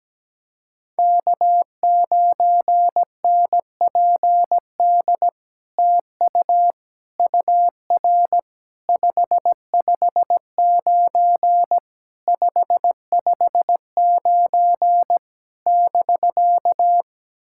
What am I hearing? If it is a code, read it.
Morse code, heard as a keyed tone, about 17 wpm, 700 Hz. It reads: K9NPD TU UR 559 559 <BK>
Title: SOTA QSO Practice 17wpm